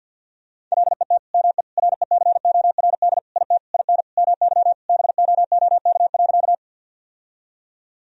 Morse code, heard as a keyed tone, about 50 wpm, 700 Hz. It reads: LET ME RECORD IT IN MY BOOK.
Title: Sentences from Top 1000 Words - Review Part 3 - 50wpm